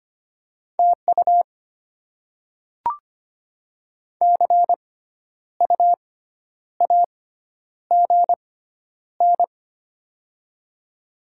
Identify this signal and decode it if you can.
Morse code, heard as a keyed tone, about 25 wpm, spaced as for 10 wpm, 690 Hz. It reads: TU CU AGN
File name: Top 20 QSO Elements - 6x Character Spacing - 25wpm